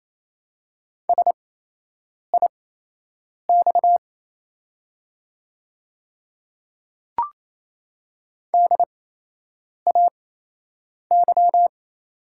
Morse code, heard as a keyed tone, about 28 wpm, spaced as for 8 wpm, 700 Hz. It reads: SIX DAY
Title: 8x Character Spacing - Top 500 Words - 28wpm